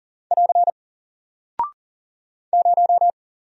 Morse code, heard as a keyed tone, about 40 wpm, 700 Hz. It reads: <AR> 0
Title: Single Character (Letter:Number:Punctuation:Prosign) 40wpm